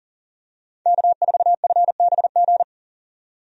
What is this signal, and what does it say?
Morse code, heard as a keyed tone, about 40 wpm, 700 Hz. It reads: K4FBG
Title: USA Call Signs - 5 Characters - 40wpm